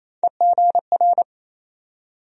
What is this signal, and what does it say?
Morse code, heard as a keyed tone, about 28 wpm, 700 Hz. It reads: EGR
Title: USA Call Sign Postfixes - 3 Letters 28wpm